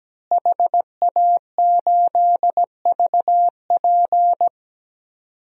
Morse code, heard as a keyed tone, about 17 wpm, 700 Hz. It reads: HA8VP
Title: DXCC-100 Call Signs - 5 Characters - Repeated 3x - 17wpm